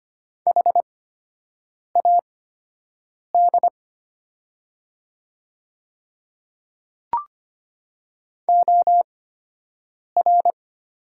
Morse code, heard as a keyed tone, about 25 wpm, spaced as for 7 wpm, 700 Hz. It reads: HAD OR
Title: 8x Character Spacing - Top 100 Words - 25wpm